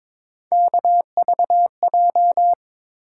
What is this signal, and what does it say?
Morse code, heard as a keyed tone, about 22 wpm, 700 Hz. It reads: KVJ
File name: practice / 22wpm / Three Letter-Number Combinations - 22wpm